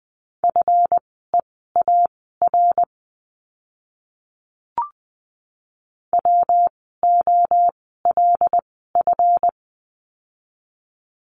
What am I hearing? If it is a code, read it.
Morse code, heard as a keyed tone, about 20 wpm, spaced as for 14 wpm, 700 Hz. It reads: FEAR WOLF
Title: 139 Lesson 23f - Letter f - Words 20wpm